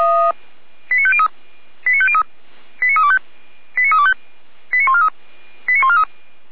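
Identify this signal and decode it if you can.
Five-tone selective call (ZVEI), heard as quick runs of five tones, tones of about 70 ms. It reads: 89503 89503 89325 89325 89143 89143
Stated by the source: ZVEI five-tone selective call 89250 received by SDR, from the feuerwehr-orlamuende fire brigade server.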